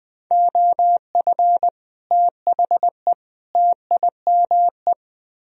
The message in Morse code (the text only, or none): OF THE TIME